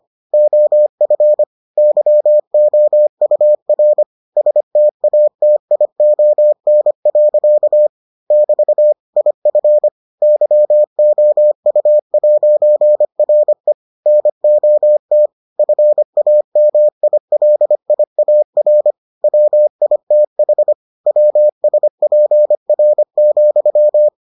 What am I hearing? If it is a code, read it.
Morse code, heard as a keyed tone, about 25 wpm, 600 Hz. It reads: OF YOUR STATION. <BT> IF YOU'RE NOT FAMILIAR WITH WSPR,